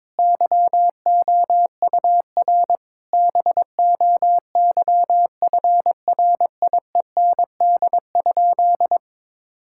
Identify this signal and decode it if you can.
Morse code, heard as a keyed tone, about 22 wpm, 700 Hz. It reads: YOUR BOYFRIEND?
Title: Sentences from Top 1000 Words - Review Part 5 - 22wpm